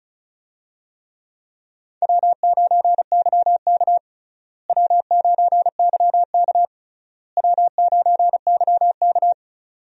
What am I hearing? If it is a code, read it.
Morse code, heard as a keyed tone, about 36 wpm, 700 Hz. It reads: W9YK W9YK W9YK
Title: USA Call Signs - 4 Characters - Repeated 3x - 35wpm